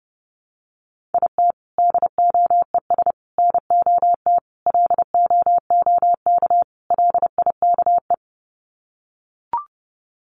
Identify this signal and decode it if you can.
Morse code, heard as a keyed tone, about 30 wpm, 700 Hz. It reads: IT DOES NOT LOOK LIKE
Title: Sets of 5 Words Encore2 30wpm